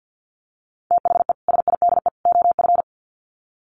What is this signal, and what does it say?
Morse code, heard as a keyed tone, about 50 wpm, 700 Hz. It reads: THE SIDE OF
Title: Sets of 3 Words 50wpm